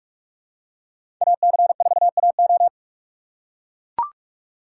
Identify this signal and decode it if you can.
Morse code, heard as a keyed tone, about 43 wpm, 700 Hz. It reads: AC4AO
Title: USA Call Signs - 5 Characters - 4x Character Spacing - 45wpm